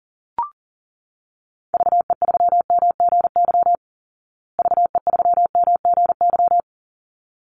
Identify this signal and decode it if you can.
Morse code, heard as a keyed tone, about 40 wpm, 700 Hz. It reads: VE3MGY VE3MGY
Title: Call Signs Repeated 3x - 40wpm